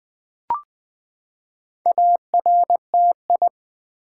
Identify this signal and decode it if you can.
Morse code, heard as a keyed tone, about 20 wpm, 700 Hz. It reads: ARTI
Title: Build A Word - 4 Letters At A Time - 10-Letter Words - 20wpm